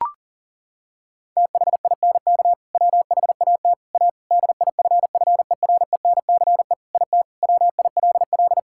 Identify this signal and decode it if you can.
Morse code, heard as a keyed tone, about 40 wpm, 700 Hz. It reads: THINK WHAT A DIFFERENCE IT WILL
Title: Sentences from Top 1000 Words - Review Part 5 - 40wpm